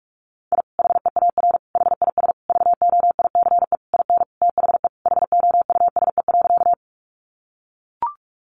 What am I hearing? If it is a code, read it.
Morse code, heard as a keyed tone, about 44 wpm, 700 Hz. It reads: I HEAR HIS VOICE IN THE HOUSE.
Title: Sentences from Top 500 Words 45wpm